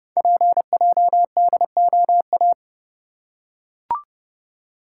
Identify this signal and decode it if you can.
Morse code, heard as a keyed tone, about 29 wpm, 700 Hz. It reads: PJDOA
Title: Code Groups - 5 Letters - 30wpm